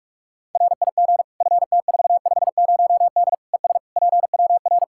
Morse code, heard as a keyed tone, about 46 wpm, 700 Hz. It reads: RIG FT450D ES PWR
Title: Practice QSO - Exchange 2 - Info - 45wpm